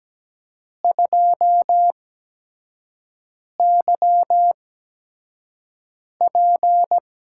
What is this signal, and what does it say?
Morse code, heard as a keyed tone, about 17 wpm, spaced as for 5 wpm, 700 Hz. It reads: 2YP
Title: USA Call Signs - 8x Character Spacing - 17wpm